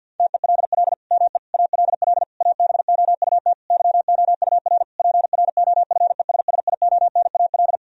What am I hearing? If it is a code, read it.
Morse code, heard as a keyed tone, about 50 wpm, 700 Hz. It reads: TELL ME ALL ABOUT YOUR PROFESSIONAL
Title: Sentences from Top 1000 Words - Review Part 4 - 50wpm